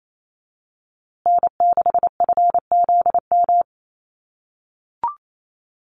Morse code, heard as a keyed tone, about 28 wpm, 700 Hz. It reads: N6FZM
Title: USA Call Signs Repeated 2x 28wpm